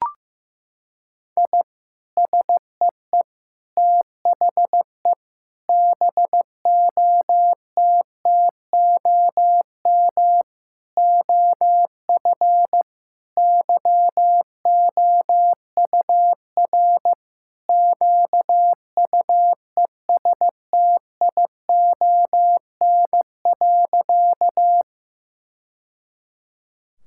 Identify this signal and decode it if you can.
Morse code, heard as a keyed tone, about 15 wpm, 700 Hz. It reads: I SEE THE BOTTOM OF YOUR QUESTION.
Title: Sentences from Top 1000 Words - Review Part 6 - 15wpm